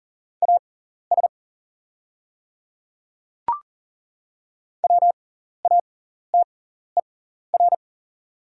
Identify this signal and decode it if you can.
Morse code, heard as a keyed tone, about 40 wpm, spaced as for 14 wpm, 690 Hz. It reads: AS WATER
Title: 6x Character Spacing - Top 100 Words - 40wpm